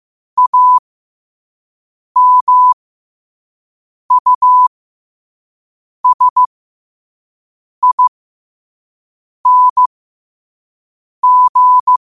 Morse code, heard as a keyed tone, about 15 wpm, spaced as for 5 wpm, 1000 Hz. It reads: AMUSING
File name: qotdfiles / 5wpm